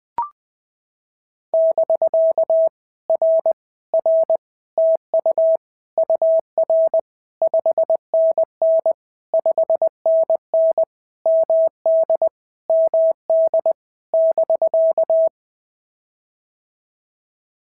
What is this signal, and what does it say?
Morse code, heard as a keyed tone, about 20 wpm, 650 Hz. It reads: <BK> R R TU UR 5NN 5NN MD MD <BK>
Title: POTA QSO Practice - 20wpm